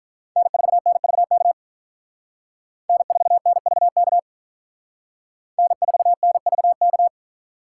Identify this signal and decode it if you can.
Morse code, heard as a keyed tone, about 45 wpm, 700 Hz. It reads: N4NVK N4NVK N4NVK